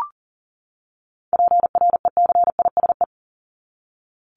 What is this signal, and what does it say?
Morse code, heard as a keyed tone, about 40 wpm, 700 Hz. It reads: PRECISE
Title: LY Suffix - Top 100 Words with Base Words 40wpm